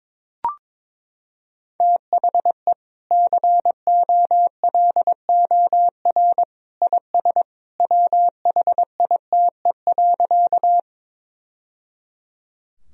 Morse code, heard as a keyed tone, about 22 wpm, 700 Hz. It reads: THE COLOR IS WHITE.
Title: Sentences from Top 1000 Words - Review Part 2 - 22wpm